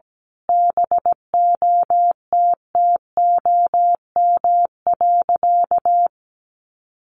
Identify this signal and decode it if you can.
Morse code, heard as a keyed tone, about 17 wpm, 700 Hz. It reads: BOTTOM.
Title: Sentences from Top 1400 Words 17wpm